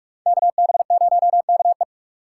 Morse code, heard as a keyed tone, about 43 wpm, 700 Hz. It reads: KB0KE
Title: USA Call Signs - 5 Characters - 6x Character Spacing - 45wpm